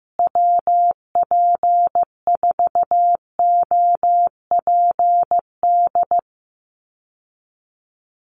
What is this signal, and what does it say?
Morse code, heard as a keyed tone, about 14 wpm, 700 Hz. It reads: WP4OPD
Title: USA Call Signs Repeated 2x 15wpm